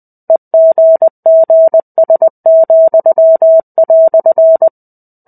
Morse code, heard as a keyed tone, about 20 wpm, 650 Hz. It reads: EGGS,"
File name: 05-Chapter-05 20 WPM